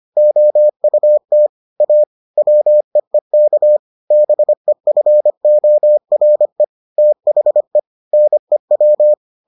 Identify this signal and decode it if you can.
Morse code, heard as a keyed tone, about 25 wpm, 600 Hz. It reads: OUT A WEEK BEFORE THE NEW